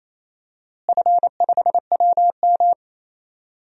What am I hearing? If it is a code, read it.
Morse code, heard as a keyed tone, about 28 wpm, 700 Hz. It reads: F5WM
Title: DXCC-100 Call Signs - 4 Characters - Repeated 3x - 28wpm